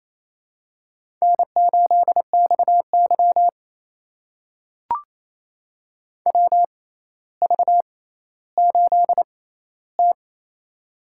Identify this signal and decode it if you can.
Morse code, heard as a keyed tone, about 28 wpm, spaced as for 14 wpm, 700 Hz. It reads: N8XY WV8T
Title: USA Call Signs - 4 Characters - 6x Character Spacing - 28wpm